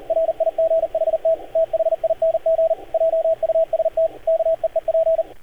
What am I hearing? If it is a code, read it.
Morse code, heard as a keyed tone, about 39 wpm, 650 Hz, with radio noise. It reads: RIGHT THING JUST KEEP
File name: archived / CRSnet